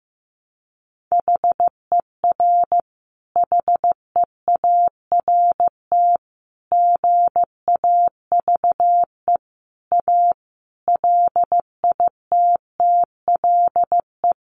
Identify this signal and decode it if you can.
Morse code, heard as a keyed tone, about 15 wpm, 700 Hz. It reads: HER HEART GAVE A LITTLE
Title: Sentences from Top 1300 Words 15wpm